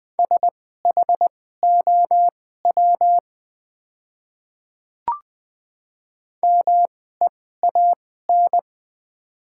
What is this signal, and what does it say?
Morse code, heard as a keyed tone, about 20 wpm, spaced as for 14 wpm, 700 Hz. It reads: SHOW MEAN
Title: 2x Character Spacing - Top 200 Words - 20wpm